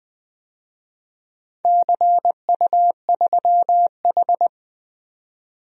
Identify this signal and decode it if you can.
Morse code, heard as a keyed tone, about 20 wpm, 700 Hz. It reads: CU3H
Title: DXCC-100 Call Signs - 4 Characters - 20wpm